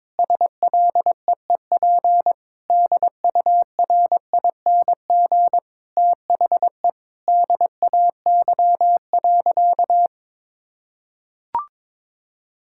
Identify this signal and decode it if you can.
Morse code, heard as a keyed tone, about 22 wpm, 700 Hz. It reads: SLEEP DURING THE DAY.
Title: Sentences from Top 1000 Words - Review Part 6 - 22wpm